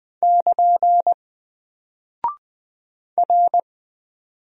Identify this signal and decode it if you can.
Morse code, heard as a keyed tone, about 20 wpm, 700 Hz. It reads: <KN> R
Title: Single Character (Letter:Number:Punctuation:Prosign) Alt 20wpm